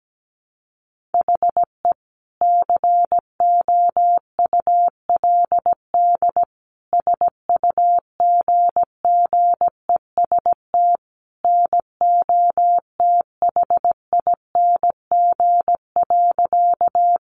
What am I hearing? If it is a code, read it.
Morse code, heard as a keyed tone, about 17 wpm, 700 Hz. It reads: HE COULD SUGGEST NOTHING.
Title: Sentences from Top 1300 Words 17wpm